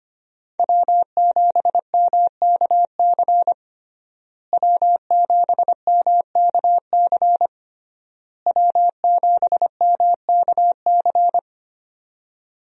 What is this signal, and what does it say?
Morse code, heard as a keyed tone, about 25 wpm, 700 Hz. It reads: W7MKC W7MKC W7MKC
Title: USA Call Signs - 5 Characters - Repeated 3x - 25wpm